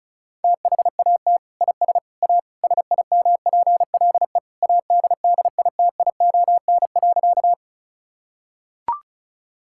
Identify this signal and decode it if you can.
Morse code, heard as a keyed tone, about 36 wpm, 700 Hz. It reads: THAT IS A SIMPLE ADDITION.
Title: Sentences from Top 1000 Words - Review Part 6 - 35wpm